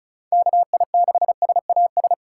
Morse code, heard as a keyed tone, about 36 wpm, 700 Hz. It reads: KI6SAS
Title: USA Call Signs - 6 Characters - 4x Character Spacing - 35wpm